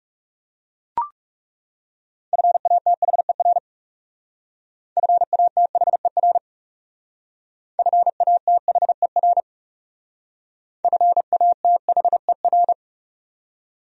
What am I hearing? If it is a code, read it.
Morse code, heard as a keyed tone, about 38 wpm, 700 Hz. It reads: FATHER FATHER FATHER FATHER